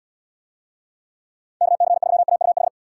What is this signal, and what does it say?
Morse code, heard as a keyed tone, about 77 wpm, 700 Hz. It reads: KC3AUH